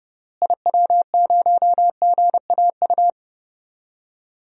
Morse code, heard as a keyed tone, about 31 wpm, 700 Hz. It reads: IW0GAU
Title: DXCC-100 Call Signs - 6 Characters - Repeated 3x - 30wpm